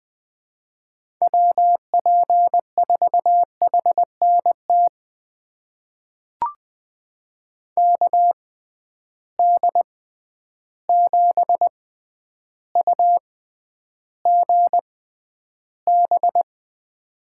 Morse code, heard as a keyed tone, about 20 wpm, spaced as for 10 wpm, 700 Hz. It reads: WP4HNT KD7UGB
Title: USA Call Signs - 6 Characters - 6x Character Spacing - 20wpm